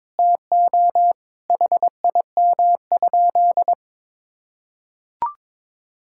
Morse code, heard as a keyed tone, about 21 wpm, 700 Hz. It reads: TO HIM?
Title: Sentences from Top 1000 Words - Review Part 6 - 22wpm